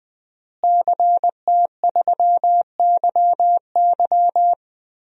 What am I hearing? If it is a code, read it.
Morse code, heard as a keyed tone, about 20 wpm, 700 Hz. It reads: CT3YY
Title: DXCC-100 Call Signs - 5 Characters - Repeated 3x - 20wpm